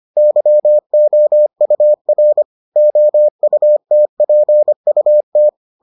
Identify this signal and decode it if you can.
Morse code, heard as a keyed tone, about 25 wpm, 600 Hz. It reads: YOUR OUTPUT